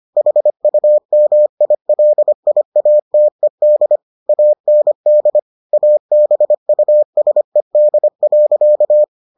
Morse code, heard as a keyed tone, about 25 wpm, 600 Hz. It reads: HUMILIATED AND ABUSED.